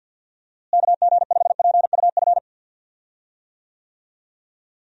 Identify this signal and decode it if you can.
Morse code, heard as a keyed tone, about 54 wpm, 700 Hz. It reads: KG5PUF
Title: USA Call Signs - 6 Characters - Repeated 2x - 50wpm